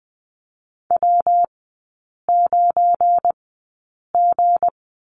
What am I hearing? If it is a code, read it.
Morse code, heard as a keyed tone, about 20 wpm, spaced as for 8 wpm, 700 Hz. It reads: W9G
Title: 172 Lesson 28e - Number 9 - Three Characters 20wpm